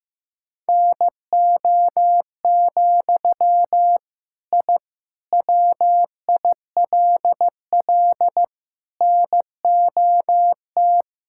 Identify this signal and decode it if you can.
Morse code, heard as a keyed tone, about 15 wpm, 700 Hz. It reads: NO, I WILL NOT
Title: Sentences from Top 1000 Words - Review Part 5 - 15wpm